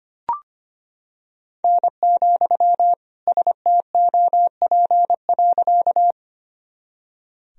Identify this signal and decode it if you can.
Morse code, heard as a keyed tone, about 25 wpm, 700 Hz. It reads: N, STOP.
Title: Sentences from Top 300 Words and Abbreviations 25wpm